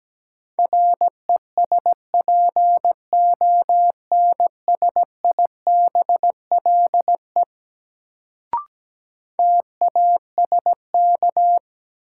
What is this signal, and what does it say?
Morse code, heard as a keyed tone, about 17 wpm, 700 Hz. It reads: RESPONSIBLE TASK
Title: Sentences from Top 1000 Words - Review Part 7 - New Word Drill - 17wpm